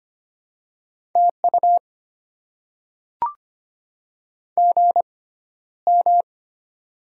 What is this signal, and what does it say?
Morse code, heard as a keyed tone, about 25 wpm, spaced as for 13 wpm, 700 Hz. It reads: TU GM
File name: Top 20 QSO Elements - 6x Character Spacing - 25wpm